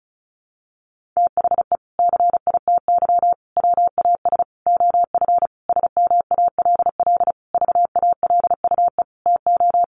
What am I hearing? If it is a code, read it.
Morse code, heard as a keyed tone, about 34 wpm, 700 Hz. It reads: THE CITY WAS OF SMALL VALUE TO
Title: Sentences from Top 1500 Words 35wpm